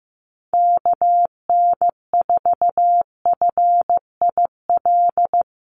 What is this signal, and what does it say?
Morse code, heard as a keyed tone, about 15 wpm, 700 Hz. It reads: KN4FIL
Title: USA Call Signs Repeated 3x 15wpm